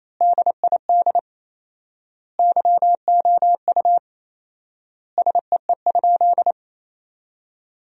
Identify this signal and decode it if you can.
Morse code, heard as a keyed tone, about 28 wpm, 700 Hz. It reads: DID YOU SEE?
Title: 4x Word-Spacing - Sentences from Top 100 Words - 28wpm